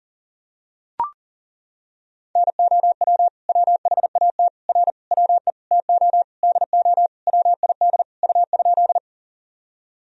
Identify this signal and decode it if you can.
Morse code, heard as a keyed tone, about 40 wpm, 700 Hz. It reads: NOW WHAT R WE TO DO WID U?
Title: Sentences from Top 100 Words and Abbreviations 40wpm